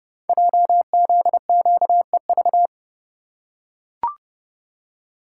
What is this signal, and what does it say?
Morse code, heard as a keyed tone, about 28 wpm, 700 Hz. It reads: JZQEV
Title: Code Groups - 5 Letters - 30wpm